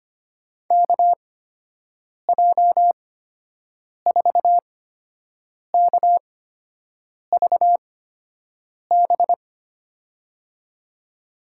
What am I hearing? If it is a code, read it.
Morse code, heard as a keyed tone, about 25 wpm, spaced as for 7 wpm, 700 Hz. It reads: KJ4KVB
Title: USA Call Signs - 6 Characters - 8x Character Spacing - 25wpm